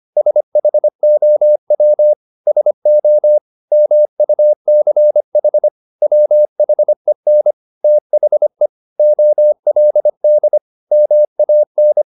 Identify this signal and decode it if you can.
Morse code, heard as a keyed tone, about 25 wpm, 600 Hz. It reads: SHOW SO MUCH WHEN THE OLD MAN
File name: the old man and the sea part 1 25wpm